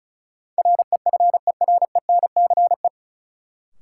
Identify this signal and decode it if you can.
Morse code, heard as a keyed tone, about 36 wpm, 700 Hz. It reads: REFERENCE
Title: Sentences from Top 1200 Words - New Word Drill Alt - 35wpm